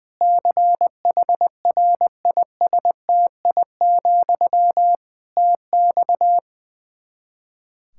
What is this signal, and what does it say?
Morse code, heard as a keyed tone, about 21 wpm, 700 Hz. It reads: CHRISTI, TX
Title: Practice QSO - Exchange 1 - Beginning - 20wpm